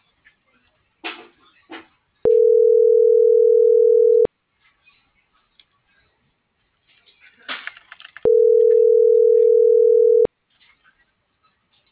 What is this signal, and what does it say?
Ringback tone (North America), 440 + 480 Hz.